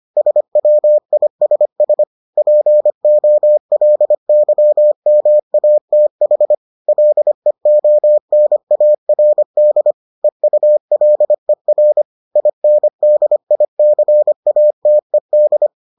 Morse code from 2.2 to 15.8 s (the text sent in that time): POLYMATH LEONARD EULER INDICATED